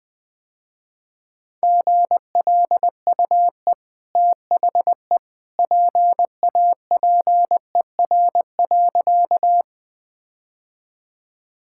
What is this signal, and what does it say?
Morse code, heard as a keyed tone, about 21 wpm, 700 Hz. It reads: GLUE THE PAPER.
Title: Easy Sentences with Hints - 20wpm